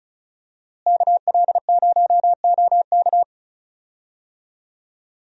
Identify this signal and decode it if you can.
Morse code, heard as a keyed tone, about 34 wpm, 700 Hz. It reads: KL0OK
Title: USA Call Signs - 5 Characters - Repeated 2x - 35wpm